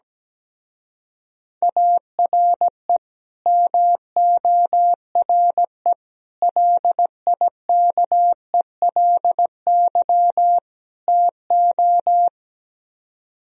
Morse code, heard as a keyed tone, about 17 wpm, 700 Hz. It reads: ARE MORE LIKELY TO
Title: Speed-Racing - Sets of 4 Words 17wpm